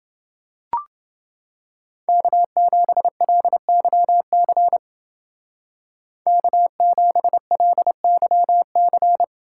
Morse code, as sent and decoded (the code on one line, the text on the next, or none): -.- --... .-.. -.-- -.-. / -.- --... .-.. -.-- -.-.
K7LYC K7LYC